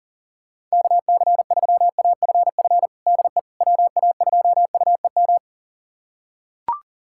Morse code, heard as a keyed tone, about 37 wpm, 690 Hz. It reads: KC3AFF DE WA2UEM
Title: USA Callsign DE Callsign - 6 Characters - 40wpm